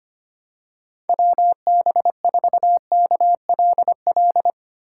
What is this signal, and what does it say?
Morse code, heard as a keyed tone, about 25 wpm, 700 Hz. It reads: WB4KLL
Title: USA Call Signs - 6 Characters - 8x Character Spacing - 25wpm